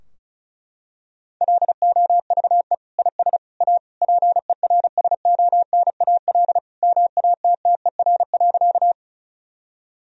Morse code, heard as a keyed tone, about 34 wpm, 700 Hz. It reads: LOVE IS A PERSONAL MATTER.
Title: Sentences from Top 1000 Words - Review Part 4 - 35wpm